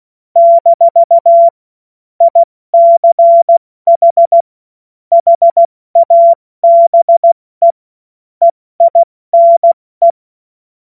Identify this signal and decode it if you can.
Morse code, heard as a keyed tone, about 16 wpm, 680 Hz. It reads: - ICH HABE EINE
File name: Morsekurs / Witze / Witz 01. Februar 2022 70 F80